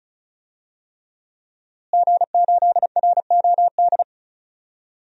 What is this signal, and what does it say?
Morse code, heard as a keyed tone, about 36 wpm, 700 Hz. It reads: G8ROD
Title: DXCC-100 Call Signs - 5 Characters - Repeated 3x - 35wpm